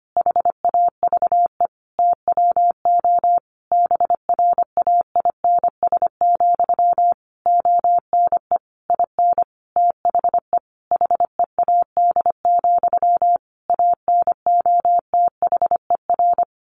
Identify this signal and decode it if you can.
Morse code, heard as a keyed tone, about 25 wpm, 700 Hz. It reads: HAVE TWO BRAINS, ONE IN THE HEAD, ANOTHER
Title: A Sound Of Thunder 25wpm